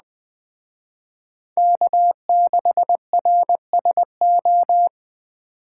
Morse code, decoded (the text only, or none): K6RSO